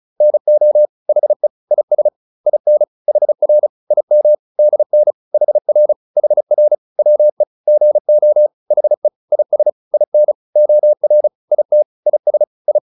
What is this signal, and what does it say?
Morse code, heard as a keyed tone, about 34 wpm, 600 Hz. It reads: NO HE IS IN HR IM DN HR HR WE GO HE IS IN OR IT IS I